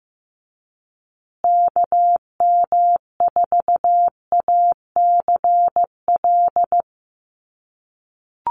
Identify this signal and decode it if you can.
Morse code, heard as a keyed tone, about 15 wpm, 700 Hz. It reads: KM4ACL
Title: USA Call Signs Repeated 3x 15wpm